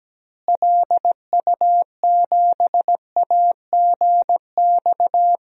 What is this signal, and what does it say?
Morse code, heard as a keyed tone, about 17 wpm, 700 Hz. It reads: LU7AGX
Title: DXCC-100 Call Signs - 6 Characters - Repeated 3x - 17wpm